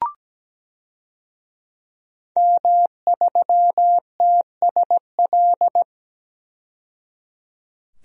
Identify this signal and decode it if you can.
Morse code, heard as a keyed tone, about 17 wpm, 700 Hz. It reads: M3TSL
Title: DXCC-100 Call Signs - 5 Characters - 17wpm